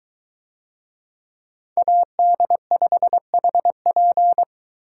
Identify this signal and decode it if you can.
Morse code, heard as a keyed tone, about 22 wpm, 700 Hz. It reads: AD5HP